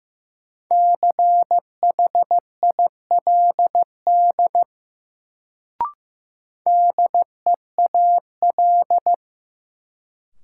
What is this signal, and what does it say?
Morse code, heard as a keyed tone, about 15 wpm, 700 Hz. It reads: CHILD DEAL
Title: Sentences from Top 800 Words - New Word Drill - 15wpm